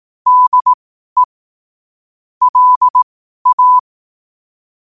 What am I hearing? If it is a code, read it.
Morse code, heard as a keyed tone, about 18 wpm, spaced as for 12 wpm, 1000 Hz. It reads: DE LA
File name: qotdfiles / 12wpm